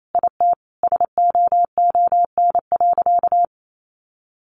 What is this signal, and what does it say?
Morse code, heard as a keyed tone, about 28 wpm, 700 Hz. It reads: IT SOON.
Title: Sentences from Top 1100 Words 28wpm